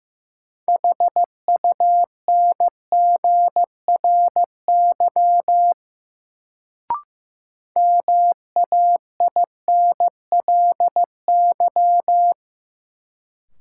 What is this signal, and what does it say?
Morse code, heard as a keyed tone, about 15 wpm, 700 Hz. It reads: HUNGRY MAINLY
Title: Sentences from Top 1400 Words - New Word Drill Alt - 15wpm